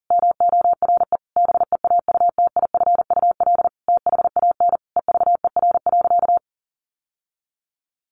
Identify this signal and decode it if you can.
Morse code, heard as a keyed tone, about 37 wpm, 700 Hz. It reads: MORE BEAUTIFUL THAN EVER.
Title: Sentences from Top 1200 Words 40wpm